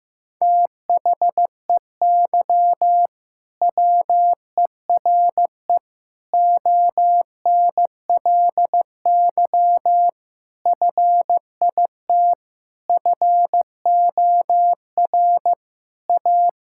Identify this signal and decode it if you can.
Morse code, heard as a keyed tone, about 15 wpm, 700 Hz. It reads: THEY WERE ONLY FIT FOR A
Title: Sentences from Top 1000 Words - Review Part 5 - 15wpm